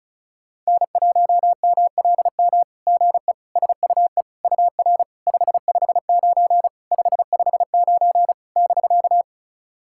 Morse code, heard as a keyed tone, about 36 wpm, 700 Hz. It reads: N1MLM GE SUE UR 559 559 <BK>
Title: SOTA QSO Practice 35wpm